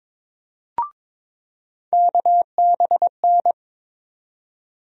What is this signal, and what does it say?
Morse code, heard as a keyed tone, about 22 wpm, 700 Hz. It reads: KBN
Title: USA Call Sign Postfixes - 3 Letters 22wpm